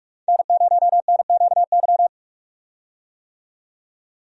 Morse code, heard as a keyed tone, about 42 wpm, 700 Hz. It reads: N0NQY
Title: USA Call Signs - 5 Characters - 45wpm